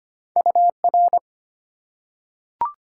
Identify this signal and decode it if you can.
Morse code, heard as a keyed tone, about 25 wpm, 700 Hz. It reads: UR
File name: Top 90 QSO Elements - 3x Character Spacing - 25wpm